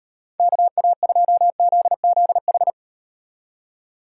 Morse code, heard as a keyed tone, about 38 wpm, 700 Hz. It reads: KA2ZZH